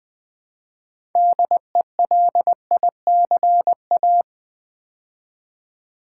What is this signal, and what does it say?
Morse code, heard as a keyed tone, about 20 wpm, 700 Hz. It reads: DELICA
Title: Build A Word - 2 Letters At A Time - 8-Letter Words - 20wpm